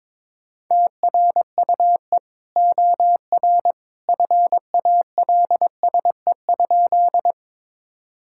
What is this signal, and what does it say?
Morse code, heard as a keyed tone, about 21 wpm, 700 Hz. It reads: TRUE OR FALSE?
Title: Sentences from Top 1600 Words 22wpm